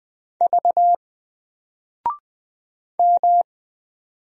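Morse code, heard as a keyed tone, about 20 wpm, 700 Hz. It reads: V M
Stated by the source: Single letter and number Morse drill at 20 wpm, from N4LCD.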